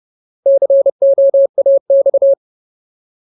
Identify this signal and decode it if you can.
Morse code, heard as a keyed tone, about 30 wpm, 550 Hz. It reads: COAX
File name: Ultimate Word List / 30WPM Ultimate